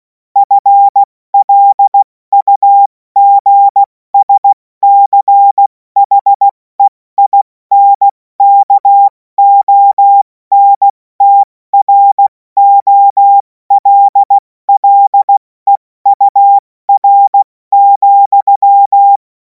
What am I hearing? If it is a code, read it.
Morse code, heard as a keyed tone, about 16 wpm, 800 Hz. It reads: FLUGSCHEINKONTROLLEUR,
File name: Morsekurs / Witze / Witz 31. März 2026 70 F80